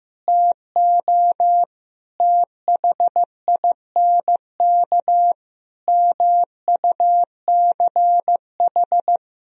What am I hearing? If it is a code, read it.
Morse code, heard as a keyed tone, about 15 wpm, 700 Hz. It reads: TO THINK MUCH
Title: Sentences from Top 1000 Words - Review Part 2 - 15wpm